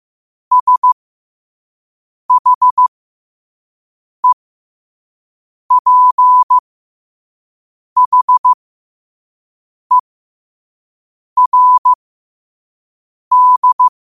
Morse code, heard as a keyed tone, about 15 wpm, spaced as for 5 wpm, 1000 Hz. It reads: SHEPHERD
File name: qotdfiles / 5wpm